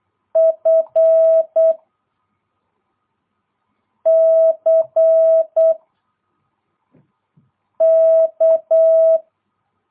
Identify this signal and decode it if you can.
Morse code, heard as a keyed tone, about 8 wpm, 650 Hz. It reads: F C K
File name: jh8cbh / 2022cw